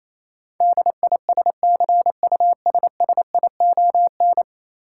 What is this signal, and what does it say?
Morse code, heard as a keyed tone, about 28 wpm, 700 Hz. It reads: DISCUSSION
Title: Sentences from Top 1100 Words - New Word Drill Alt - 28wpm